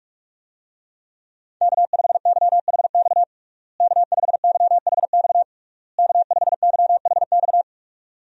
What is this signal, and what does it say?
Morse code, heard as a keyed tone, about 46 wpm, 700 Hz. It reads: K5YHX K5YHX K5YHX
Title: USA Call Signs - 5 Characters - Repeated 3x - 45wpm